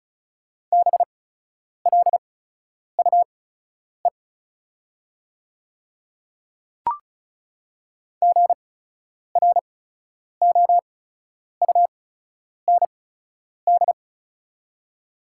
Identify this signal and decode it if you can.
Morse code, heard as a keyed tone, about 36 wpm, spaced as for 10 wpm, 690 Hz. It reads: BLUE GROUND
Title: 8x Character Spacing - Top 500 Words - 35wpm